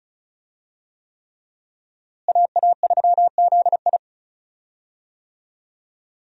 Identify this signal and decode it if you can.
Morse code, heard as a keyed tone, about 34 wpm, 700 Hz. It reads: AA3ZI
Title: USA Call Signs - 5 Characters - Repeated 3x - 35wpm